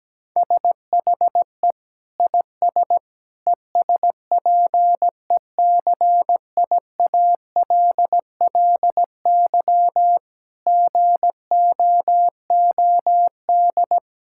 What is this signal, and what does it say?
Morse code, heard as a keyed tone, about 17 wpm, 700 Hz. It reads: SHE IS ESPECIALLY GOOD
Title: Sentences from Top 1000 Words - Review Part 6 - 17wpm